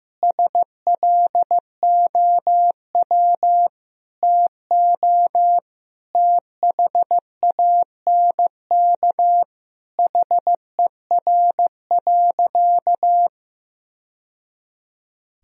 Morse code, heard as a keyed tone, about 15 wpm, 700 Hz. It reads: SLOW TO THANK HER.
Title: Sentences from Top 1000 Words - Review Part 2 - 15wpm